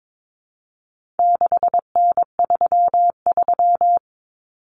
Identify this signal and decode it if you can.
Morse code, heard as a keyed tone, about 22 wpm, 700 Hz. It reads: 6N33
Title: Common Cut Numbers - 4-Digit 22wpm